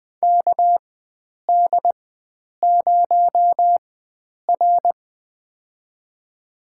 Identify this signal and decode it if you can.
Morse code, heard as a keyed tone, about 20 wpm, spaced as for 9 wpm, 700 Hz. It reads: KD0R
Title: USA Call Signs - 4 Characters - 4x Character Spacing - 20wpm